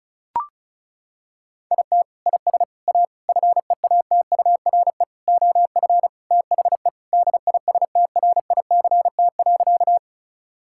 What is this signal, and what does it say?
Morse code, heard as a keyed tone, about 36 wpm, 700 Hz. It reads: IT IS A FEATURE OF THE DISTRICT.
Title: Sentences from Top 1600 Words 35wpm